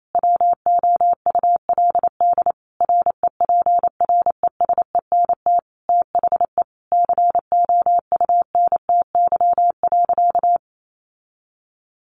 Morse code, heard as a keyed tone, about 28 wpm, 700 Hz. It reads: WOULD REPRESENT THE COUNTY.
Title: Sentences from Top 1200 Words 28wpm